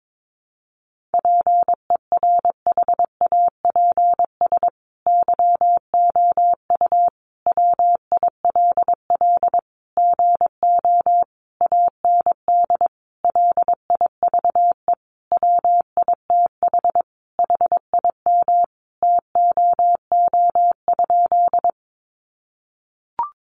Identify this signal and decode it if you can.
Morse code, heard as a keyed tone, about 22 wpm, 700 Hz. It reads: PERHAPS YOU WILL GO AND LIVE WITH HIM TOO?
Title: Sentences from Top 500 Words 22wpm